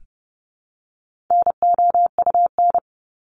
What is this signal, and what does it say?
Morse code, heard as a keyed tone, about 30 wpm, 700 Hz. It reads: NOUN
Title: Top 500 Words 30wpm